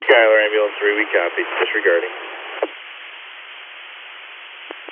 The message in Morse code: none

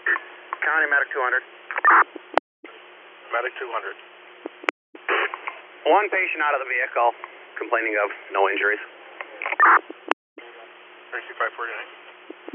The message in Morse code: none

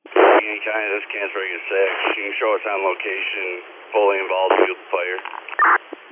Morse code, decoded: none

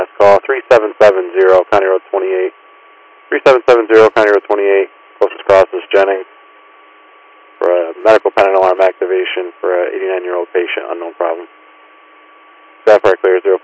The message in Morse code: none

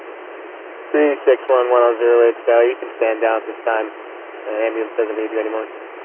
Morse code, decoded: none

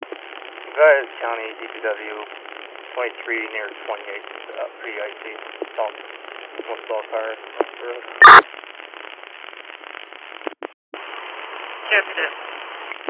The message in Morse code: none

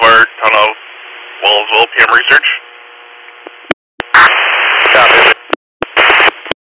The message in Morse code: none

